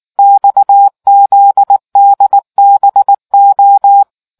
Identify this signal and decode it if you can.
Morse code, heard as a keyed tone, about 19 wpm, 800 Hz. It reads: XZDBO